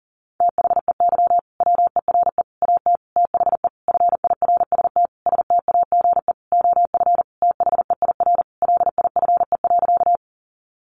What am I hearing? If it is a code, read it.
Morse code, heard as a keyed tone, about 40 wpm, 700 Hz. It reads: THEY WERE AT THE FIRST STAGE OF THEIR LIFE.
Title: Sentences from Top 1200 Words 40wpm